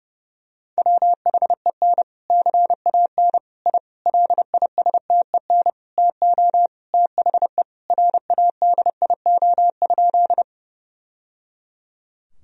Morse code, heard as a keyed tone, about 32 wpm, 700 Hz. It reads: WHEN CAN I LISTEN TO THE RADIO?
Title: Sentences from Top 1000 Words - Review Part 3 - 30wpm